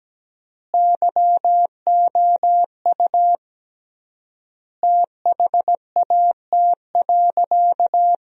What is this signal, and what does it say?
Morse code, heard as a keyed tone, about 17 wpm, 700 Hz. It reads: YOU THAT.
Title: 3x Word-Spacing - Sentences from Top 300 Words - 17wpm